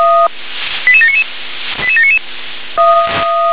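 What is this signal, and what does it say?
Five-tone selective call (ZVEI), heard as quick runs of five tones, tones of about 70 ms. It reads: 88799 88799
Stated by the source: Five-tone ZVEI selective call received on an SDR.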